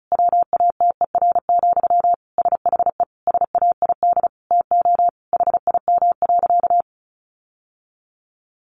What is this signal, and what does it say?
Morse code, heard as a keyed tone, about 34 wpm, 700 Hz. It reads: WATER, SHE SAID TO HIM.
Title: Sentences from Top 800 Words 35wpm